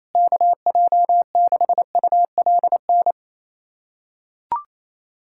Morse code, heard as a keyed tone, about 27 wpm, 700 Hz. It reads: KJ6ULN
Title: USA Callsign DE Callsign - 6 Characters - 28wpm